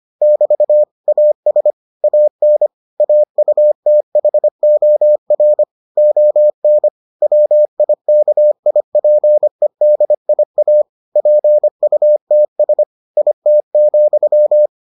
Morse code, heard as a keyed tone, about 25 wpm, 600 Hz. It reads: <BT> AS AN AUTHOR ON WIKIPEDIA PUTS IT,